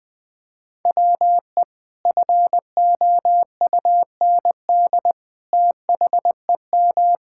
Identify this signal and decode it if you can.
Morse code, heard as a keyed tone, about 21 wpm, 700 Hz. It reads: WE FOUND THEM
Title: Sentences from Top 1000 Words - Review Part 5 - 20wpm